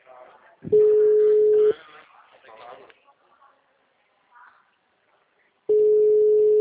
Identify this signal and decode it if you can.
Ringback tone (Europe, 425 Hz), with speech.
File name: out- -2010- - - .17966